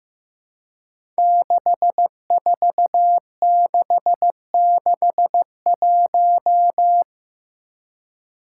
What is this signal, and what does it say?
Morse code, heard as a keyed tone, about 15 wpm, 700 Hz. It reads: 64661
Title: Five Challenging Letter-Number Combinations 15wpm